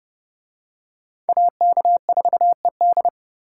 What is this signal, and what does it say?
Morse code, heard as a keyed tone, about 29 wpm, 700 Hz. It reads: AK4ED